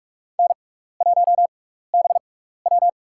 Morse code, heard as a keyed tone, about 46 wpm, spaced as for 16 wpm, 700 Hz. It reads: N1BW
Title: USA Call Signs - 4 Characters - 6x Character Spacing - 45wpm